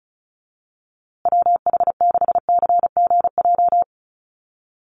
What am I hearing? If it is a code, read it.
Morse code, heard as a keyed tone, about 34 wpm, 700 Hz. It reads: WH6CGJ
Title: USA Call Signs Repeated 3x 35wpm